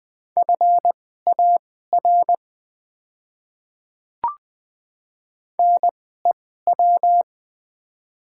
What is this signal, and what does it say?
Morse code, heard as a keyed tone, about 21 wpm, spaced as for 15 wpm, 700 Hz. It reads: FAR NEW
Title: 2x Character Spacing - Top 300 Words - 20wpm